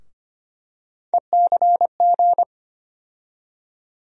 Morse code, heard as a keyed tone, about 25 wpm, 700 Hz. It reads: ECG
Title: USA Call Sign Postfixes - 3 Letters 25wpm